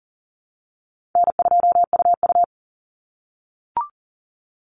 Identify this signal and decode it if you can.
Morse code, heard as a keyed tone, about 40 wpm, 700 Hz. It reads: N2UU
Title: Call Signs - 40wpm